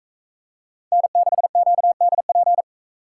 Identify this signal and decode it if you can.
Morse code, heard as a keyed tone, about 43 wpm, 700 Hz. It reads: N6QDP